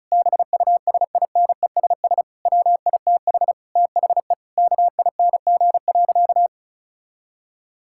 Morse code, heard as a keyed tone, about 36 wpm, 690 Hz. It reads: BUSINESS WITH THE KING.
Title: Sentences from Top 1000 Words - Review Part 6 - 35wpm